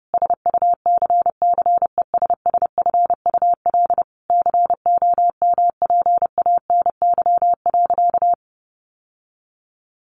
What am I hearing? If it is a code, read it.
Morse code, heard as a keyed tone, about 30 wpm, 700 Hz. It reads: SUCCESSFUL COMPANY.
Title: Sentences from Top 1100 Words 30wpm